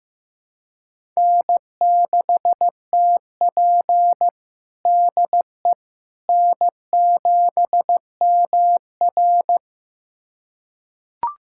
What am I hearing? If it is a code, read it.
Morse code, heard as a keyed tone, about 16 wpm, 700 Hz. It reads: N6TP DE N7MR
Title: USA Callsign DE Callsign - 4 Characters - 15wpm